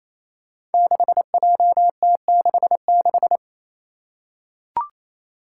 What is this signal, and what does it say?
Morse code, heard as a keyed tone, about 27 wpm, 700 Hz. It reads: 6JT66
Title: Five Challenging Letter-Number Combinations 28wpm